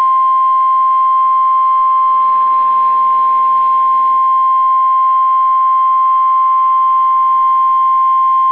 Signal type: Emergency Alert System (SAME) tones